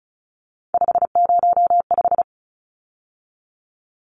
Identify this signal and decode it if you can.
Morse code, heard as a keyed tone, about 34 wpm, 700 Hz. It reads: 505
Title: Numbers - 3-Digit 35wpm